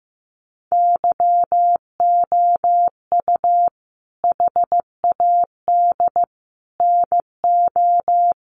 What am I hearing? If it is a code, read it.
Morse code, heard as a keyed tone, about 15 wpm, 700 Hz. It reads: YOU HAD NO
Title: Sentences from Top 1300 Words 15wpm